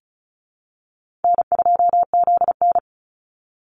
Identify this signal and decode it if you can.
Morse code, heard as a keyed tone, about 34 wpm, 700 Hz. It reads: N2ZN
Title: Easy Call Signs Repeated 3x - 35wpm